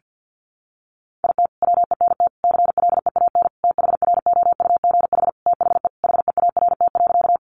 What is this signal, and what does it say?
Morse code, heard as a keyed tone, about 50 wpm, 700 Hz. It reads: IT WENT CLEAN THROUGH THE HEART.
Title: Sentences from Top 1200 Words 50wpm